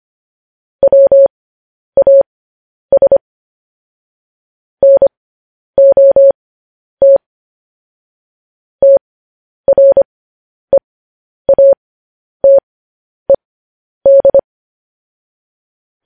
Morse code, heard as a keyed tone, about 25 wpm, spaced as for 10 wpm, 550 Hz. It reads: WAS NOT TREATED
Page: news-headlines-in-morse-code-at-10-wpm